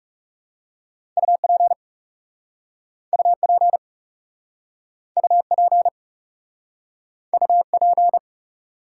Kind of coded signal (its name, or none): Morse code